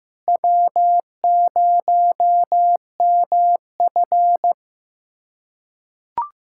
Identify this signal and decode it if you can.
Morse code, heard as a keyed tone, about 15 wpm, 700 Hz. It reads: W0MF
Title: USA Callsign DE Callsign - 4 Characters - 15wpm